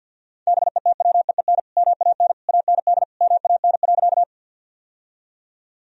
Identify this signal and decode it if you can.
Morse code, heard as a keyed tone, about 48 wpm, 700 Hz. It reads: BETWEEN MAN AND MAN.
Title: Sentences from Top 1000 Words - Review Part 1 - 50wpm